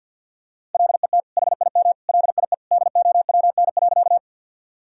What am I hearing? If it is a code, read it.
Morse code, heard as a keyed tone, about 50 wpm, 700 Hz. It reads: LET HIM LIE DOWN.
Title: Sentences from Top 1600 Words 50wpm